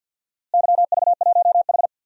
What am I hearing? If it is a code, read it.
Morse code, heard as a keyed tone, about 46 wpm, 700 Hz. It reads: YV1H
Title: DXCC-100 Call Signs - 4 Characters - Repeated 2x - 50wpm